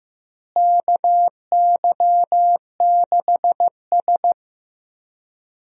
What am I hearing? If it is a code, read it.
Morse code, heard as a keyed tone, about 15 wpm, 700 Hz. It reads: KY6S
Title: USA Call Signs - 4 Characters - Repeated 2x - 15wpm